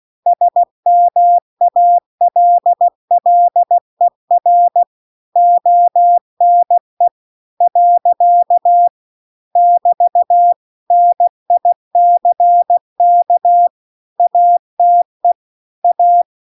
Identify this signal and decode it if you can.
Morse code, heard as a keyed tone, about 16 wpm, 700 Hz. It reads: SMALLER ONE . <BT> NICK ATE A
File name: CW-The-Big-Twoheaded-River-2-16WPM